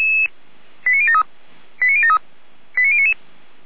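Five-tone selective call (ZVEI), heard as quick runs of five tones, tones of about 70 ms. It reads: 89073 89073 89099
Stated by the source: ZVEI five-tone selective call received by SDR, fire brigade Feuerwehr Orlamünde.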